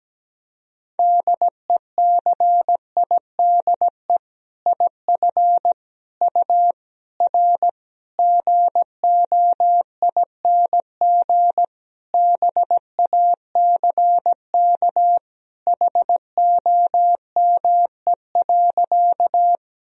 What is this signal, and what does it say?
Morse code, heard as a keyed tone, about 17 wpm, 700 Hz. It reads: DECIDE IF U R GOING BACK HOME.
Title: Sentences from Top 500 Words and Abbreviations 17wpm